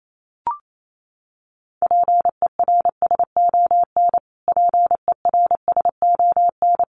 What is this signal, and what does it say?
Morse code, heard as a keyed tone, about 28 wpm, 700 Hz. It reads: PERSON PERSON
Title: Top 500 Words Repeated 28wpm